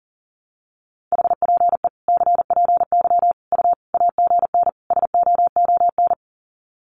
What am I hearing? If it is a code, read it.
Morse code, heard as a keyed tone, about 40 wpm, 700 Hz. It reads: HPE CPY U AGN SOON
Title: Common QSO Phrases 40wpm